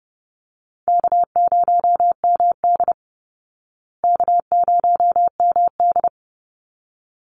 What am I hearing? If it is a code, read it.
Morse code, heard as a keyed tone, about 30 wpm, 700 Hz. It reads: K0MD K0MD
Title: Call Signs Repeated 3x - 30wpm